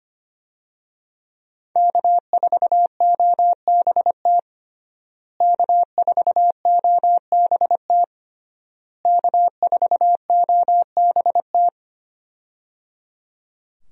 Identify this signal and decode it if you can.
Morse code, heard as a keyed tone, about 25 wpm, 700 Hz. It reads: K4OBT K4OBT K4OBT
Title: USA Call Signs - 5 Characters - Repeated 3x - 25wpm